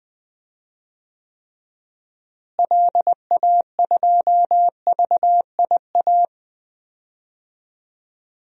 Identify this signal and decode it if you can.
Morse code, heard as a keyed tone, about 20 wpm, 700 Hz. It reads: LA2VIA
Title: DXCC-100 Call Signs - 6 Characters - Repeated 3x - 20wpm